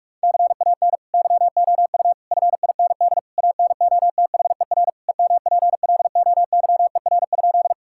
Morse code, heard as a keyed tone, about 46 wpm, 700 Hz. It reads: CAN YOU FIND ANOTHER EMPLOYER?
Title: Sentences from Top 1600 Words 45wpm